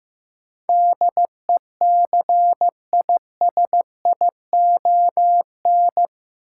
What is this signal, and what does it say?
Morse code, heard as a keyed tone, about 16 wpm, 700 Hz. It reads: DECISION
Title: Sentences from Top 1000 Words - New Word Drill - 15wpm